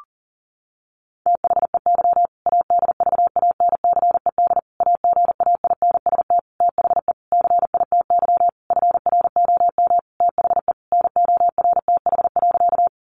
Morse code, heard as a keyed tone, about 40 wpm, 700 Hz. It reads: THEY ADVANCED AGAINST THE CITY FROM THE NORTH.
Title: Sentences from Top 1200 Words 40wpm